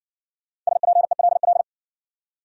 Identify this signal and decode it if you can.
Morse code, heard as a keyed tone, about 60 wpm, 700 Hz. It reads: SPELL